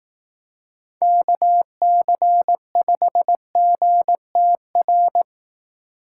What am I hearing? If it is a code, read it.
Morse code, heard as a keyed tone, about 19 wpm, 700 Hz. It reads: KC5GTR
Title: Speed Racing - USA 2x3 Call Signs 15wpm